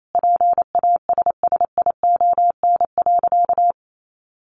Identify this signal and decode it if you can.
Morse code, heard as a keyed tone, about 28 wpm, 700 Hz. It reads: PASSION.
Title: Sentences from Top 1300 Words 28wpm